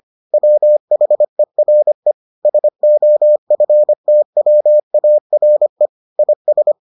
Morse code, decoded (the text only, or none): WHERE SOFTWARE IS